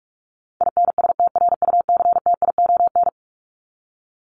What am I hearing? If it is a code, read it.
Morse code, heard as a keyed tone, about 44 wpm, 700 Hz. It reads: INSTRUCTION
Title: Top 1751-2000 Words 45wpm